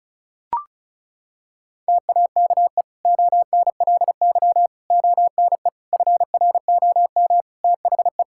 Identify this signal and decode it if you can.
Morse code, heard as a keyed tone, about 36 wpm, 700 Hz. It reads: TAKE ONLY ONE FROM THE
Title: Sentences from Top 1000 Words - Review Part 5 - 35wpm